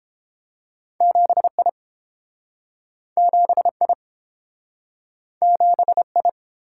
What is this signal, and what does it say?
Morse code, heard as a keyed tone, about 31 wpm, 700 Hz. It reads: 7I 7I 7I